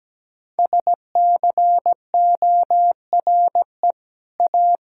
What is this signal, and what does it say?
Morse code, heard as a keyed tone, about 17 wpm, 700 Hz. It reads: SCORE A
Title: Sentences from Top 1600 Words 17wpm